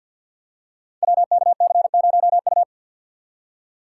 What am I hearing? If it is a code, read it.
Morse code, heard as a keyed tone, about 51 wpm, 700 Hz. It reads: WKC0U
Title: Code Groups - 5 Letter:Numbers - 50wpm